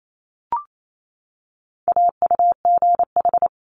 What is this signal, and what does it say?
Morse code, heard as a keyed tone, about 28 wpm, 700 Hz. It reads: AUGH
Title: Multi-letter Phonemes 28wpm